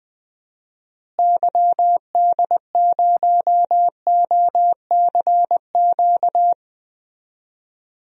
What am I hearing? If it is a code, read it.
Morse code, heard as a keyed tone, about 20 wpm, 700 Hz. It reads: YD0OCQ
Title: DXCC-100 Call Signs - 6 Characters - Repeated 3x - 20wpm